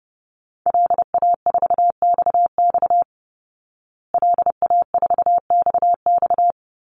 Morse code, heard as a keyed tone, about 30 wpm, 700 Hz. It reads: LA4XX LA4XX
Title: Call Signs Repeated 2x - 30wpm